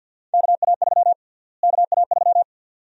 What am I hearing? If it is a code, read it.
Morse code, heard as a keyed tone, about 52 wpm, 700 Hz. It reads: KA3 KA3
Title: USA Call Sign Prefixes - Repeated 2x - 50wpm